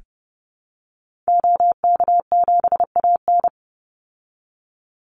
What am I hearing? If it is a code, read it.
Morse code, heard as a keyed tone, about 30 wpm, 700 Hz. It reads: OK7AN
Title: Call Signs Repeated 2x - 30wpm